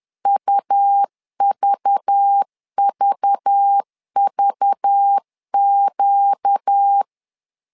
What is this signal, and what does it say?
Morse code, heard as a keyed tone, about 10 wpm, 800 Hz. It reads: UVVVQ
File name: LEZIONI / 5 LBVC / 5 LBV 10wpm 2248